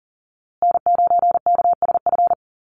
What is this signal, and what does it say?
Morse code, heard as a keyed tone, about 40 wpm, 700 Hz. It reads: N9KSF
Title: USA Call Signs Repeated 2x 40wpm